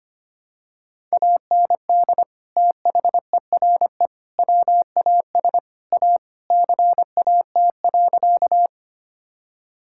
Morse code, heard as a keyed tone, about 25 wpm, 700 Hz. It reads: AND THERE WAS A CAT.
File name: practice / 25wpm / Sentences with 4-Word Phrases - 25wpm